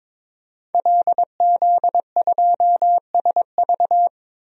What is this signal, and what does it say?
Morse code, heard as a keyed tone, about 22 wpm, 700 Hz. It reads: LZ2SV
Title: DXCC-100 Call Signs - 5 Characters - 22wpm